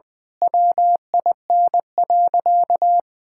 Morse code, heard as a keyed tone, about 20 wpm, 700 Hz. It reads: WIN.
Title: Sentences with 8-Word Phrases - 20wpm